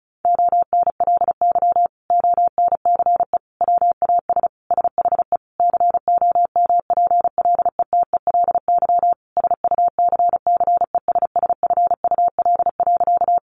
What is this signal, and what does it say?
Morse code, heard as a keyed tone, about 34 wpm, 700 Hz. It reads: ONLY ONCE WAS SHE COMPLETELY SUCCESSFUL.
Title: Sentences from Top 1100 Words 35wpm